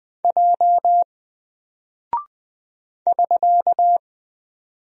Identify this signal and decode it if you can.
Morse code, heard as a keyed tone, about 20 wpm, 700 Hz. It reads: J <SK>
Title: Single Character (Letter:Number:Punctuation:Prosign) Alt 20wpm